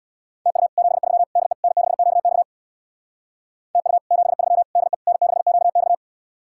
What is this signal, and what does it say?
Morse code, heard as a keyed tone, about 63 wpm, 700 Hz. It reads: TU 73 DE N6QX TU 73 DE N6QX